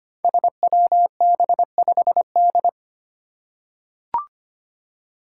Morse code, heard as a keyed tone, about 25 wpm, 700 Hz. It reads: SWB5D
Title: Five Challenging Letter-Number Combinations 25wpm